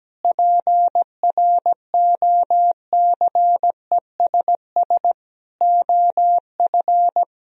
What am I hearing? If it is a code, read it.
Morse code, heard as a keyed tone, about 17 wpm, 700 Hz. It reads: PROCESS OF
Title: Sentences from Top 1000 Words - Review Part 4 - 17wpm